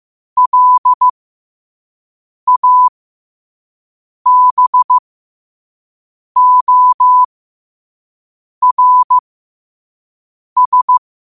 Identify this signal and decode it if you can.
Morse code, heard as a keyed tone, about 15 wpm, spaced as for 5 wpm, 1000 Hz. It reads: LABORS
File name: qotdfiles / 5wpm